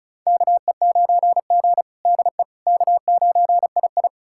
Morse code, heard as a keyed tone, about 34 wpm, 700 Hz. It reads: KE9G DE K9II
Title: USA Callsign DE Callsign - 4 Characters - 35wpm